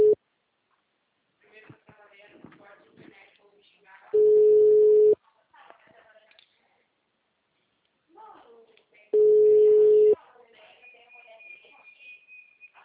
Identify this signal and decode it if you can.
Ringback tone (Europe, 425 Hz), with speech.